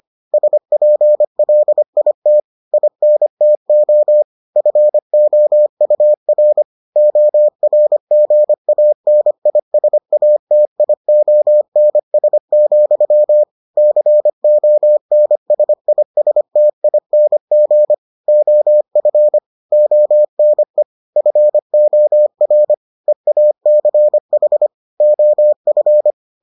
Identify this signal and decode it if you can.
Morse code, heard as a keyed tone, about 25 wpm, 600 Hz. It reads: SPLIT INTO FOUR ORGANISATIONS, CONSISTING OF ONE FOR EACH OF